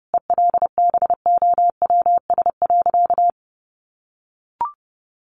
Morse code, heard as a keyed tone, about 30 wpm, 700 Hz. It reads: ELBOWS.
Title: A Sound Of Thunder 30wpm